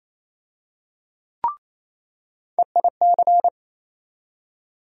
Morse code, heard as a keyed tone, about 28 wpm, 700 Hz. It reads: EIC
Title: USA Call Sign Postfixes - 3 Letters 28wpm